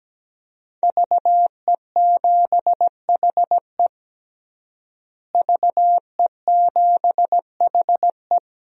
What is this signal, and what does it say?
Morse code, heard as a keyed tone, about 17 wpm, 700 Hz. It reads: VE7HE VE7HE
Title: DXCC-100 Call Signs - 5 Characters - Repeated 2x - 17wpm